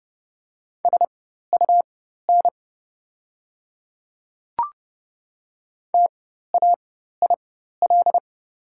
Morse code, heard as a keyed tone, about 30 wpm, spaced as for 14 wpm, 700 Hz. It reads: SUN TAIL
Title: 4x Character Spacing - Top 500 Words - 30wpm